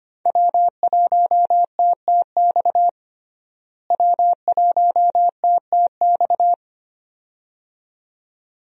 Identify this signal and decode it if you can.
Morse code, heard as a keyed tone, about 25 wpm, 700 Hz. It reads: W1TTX W1TTX
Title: USA Call Signs - 5 Characters - Repeated 2x - 25wpm